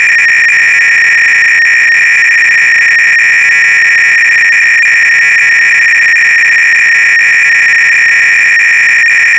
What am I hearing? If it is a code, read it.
300-baud modem data.